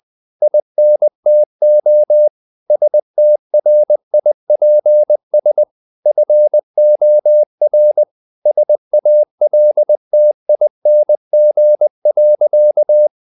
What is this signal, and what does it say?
Morse code, heard as a keyed tone, about 20 wpm, 600 Hz. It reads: INTO STRIPS FOR SALTING.
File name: the old man and the sea / the old man and the sea part 1 20wpm